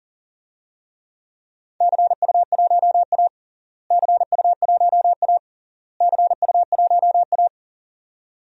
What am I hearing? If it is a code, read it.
Morse code, heard as a keyed tone, about 40 wpm, 700 Hz. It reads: CU1A CU1A CU1A
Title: DXCC-100 Call Signs - 4 Characters - Repeated 3x - 40wpm